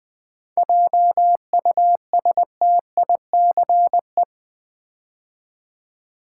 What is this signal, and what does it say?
Morse code, heard as a keyed tone, about 20 wpm, 700 Hz. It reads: JUSTICE
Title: 7 Letter Words - First 3 Letters Given - 20wpm